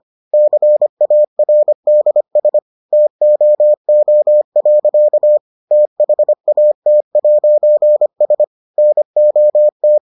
Morse code, heard as a keyed tone, about 25 wpm, 600 Hz. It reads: CARDS TOO. THAT'S NOT